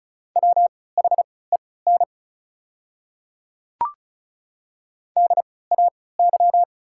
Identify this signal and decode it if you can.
Morse code, heard as a keyed tone, about 36 wpm, spaced as for 20 wpm, 690 Hz. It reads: WHEN DAY
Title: 3x Character Spacing - Top 100 Words - 35wpm